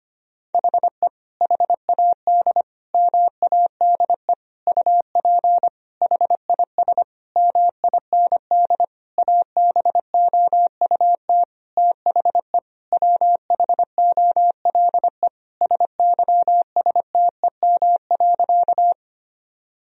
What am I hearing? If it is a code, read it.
Morse code, heard as a keyed tone, about 25 wpm, 700 Hz. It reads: HE HAD MADE UP HIS MIND ABOUT THE WHOLE SYSTEM.
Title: Sentences from Top 1000 Words - Review Part 3 - 25wpm